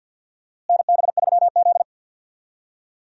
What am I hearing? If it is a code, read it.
Morse code, heard as a keyed tone, about 47 wpm, 700 Hz. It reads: NB3Z
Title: USA Call Signs - 4 Characters - Repeated 3x - 50wpm